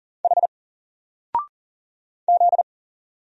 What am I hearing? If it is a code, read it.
Morse code, heard as a keyed tone, about 40 wpm, 700 Hz. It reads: H Z
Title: Single Letter-Number Alt 40wpm